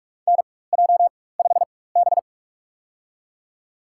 Morse code, heard as a keyed tone, about 46 wpm, spaced as for 21 wpm, 700 Hz. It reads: NJ5B
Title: USA Call Signs - 4 Characters - 4x Character Spacing - 45wpm